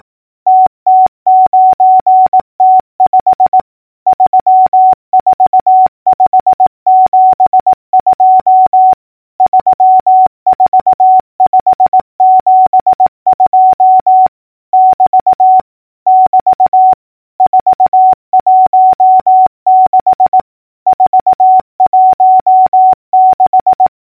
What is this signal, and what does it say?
Morse code, heard as a keyed tone, about 18 wpm, 750 Hz. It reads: TT9T5 34572 34572 <BT> <BT> 416 416
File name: numbers / V2-morse-clicks